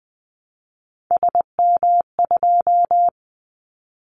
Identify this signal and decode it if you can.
Morse code, heard as a keyed tone, about 20 wpm, 700 Hz. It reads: SM2
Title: Callsign Prefixes for Top 25 DXCC - 20wpm